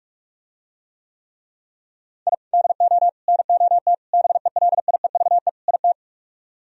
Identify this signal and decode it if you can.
Morse code, heard as a keyed tone, about 48 wpm, 700 Hz. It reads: I DO NOT BELIEVE IT